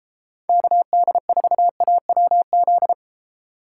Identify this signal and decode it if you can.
Morse code, heard as a keyed tone, about 33 wpm, 700 Hz. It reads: KD4AWZ